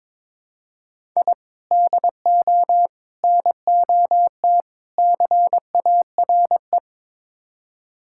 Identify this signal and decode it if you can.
Morse code, heard as a keyed tone, about 21 wpm, 700 Hz. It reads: I DO NOT CARE
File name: practice / 22wpm / Build A Phrase - 5-Word Phrases - 22wpm